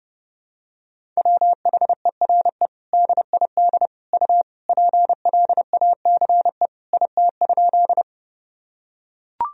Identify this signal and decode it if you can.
Morse code, heard as a keyed tone, about 30 wpm, 700 Hz. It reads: WHERE DID U PLACE IT?
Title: Sentences from Top 200 Words and Abbreviations 30wpm